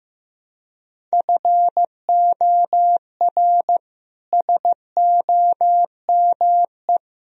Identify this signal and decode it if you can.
Morse code, heard as a keyed tone, about 15 wpm, 700 Hz. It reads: FOR SOME
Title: Sentences from Top 1000 Words - Review Part 2 - 15wpm